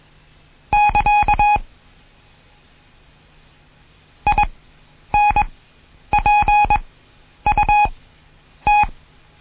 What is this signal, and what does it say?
Morse code, heard as a keyed tone, about 21 wpm, spaced as for 9 wpm, 830 Hz. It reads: <KA> INPUT
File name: Morsen / Kurs A 20